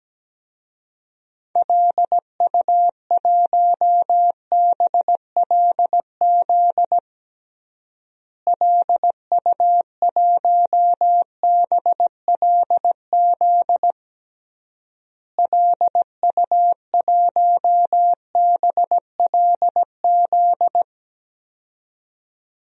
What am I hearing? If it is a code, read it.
Morse code, heard as a keyed tone, about 17 wpm, 700 Hz. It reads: LU1BLZ LU1BLZ LU1BLZ
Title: DXCC-100 Call Signs - 6 Characters - Repeated 3x - 17wpm